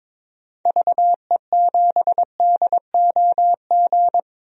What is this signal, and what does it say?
Morse code, heard as a keyed tone, about 22 wpm, 700 Hz. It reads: VE7DOG
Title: DXCC-100 Call Signs - 6 Characters - 22wpm